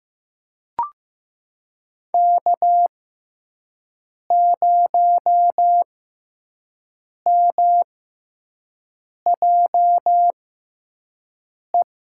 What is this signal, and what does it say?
Morse code, heard as a keyed tone, about 15 wpm, spaced as for 5 wpm, 700 Hz. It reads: K0MJE
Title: USA Call Signs - 5 Characters - 6x Character Spacing - 15wpm